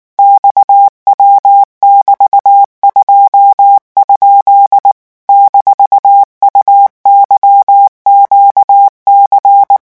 Morse code, heard as a keyed tone, about 19 wpm, 800 Hz. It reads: XW<BT>2? -UYQC
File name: morse mp3 / revision17-vitesse 18 mots